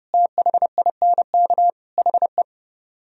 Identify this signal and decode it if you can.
Morse code, heard as a keyed tone, about 33 wpm, 700 Hz. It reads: THINK HE